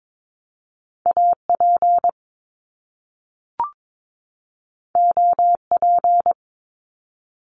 Morse code, heard as a keyed tone, about 22 wpm, 700 Hz. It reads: AP OP
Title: Top Two Letter Combinations in Words 22wpm